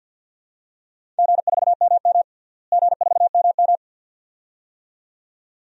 Morse code, heard as a keyed tone, about 52 wpm, 700 Hz. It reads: G4MM G4MM
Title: DXCC-100 Call Signs - 4 Characters - Repeated 2x - 50wpm